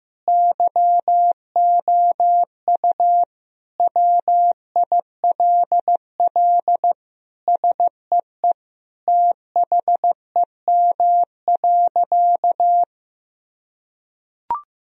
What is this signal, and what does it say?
Morse code, heard as a keyed tone, about 15 wpm, 700 Hz. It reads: YOU WILL SEE THEM.
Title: Sentences from Top 1000 Words - Review Part 2 - 15wpm